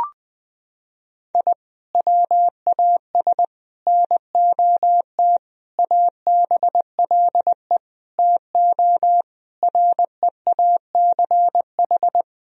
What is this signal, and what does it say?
Morse code, heard as a keyed tone, about 20 wpm, 700 Hz. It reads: I WAS NOT ABLE TO REACH
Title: Sentences from Top 1000 Words - Review Part 2 - 20wpm